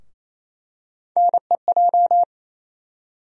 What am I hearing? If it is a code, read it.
Morse code, heard as a keyed tone, about 28 wpm, 700 Hz. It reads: NEJ
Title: USA Call Sign Postfixes - 3 Letters 28wpm